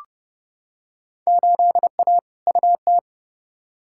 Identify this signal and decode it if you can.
Morse code, heard as a keyed tone, about 30 wpm, 700 Hz. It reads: 8A UT
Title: Field Day Exchanges Encore - 30wpm